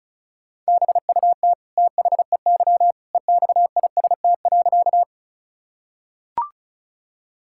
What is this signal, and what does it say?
Morse code, heard as a keyed tone, about 36 wpm, 700 Hz. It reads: BUT THEY EXIST.
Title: Sentences from Top 1000 Words - Review Part 5 - 35wpm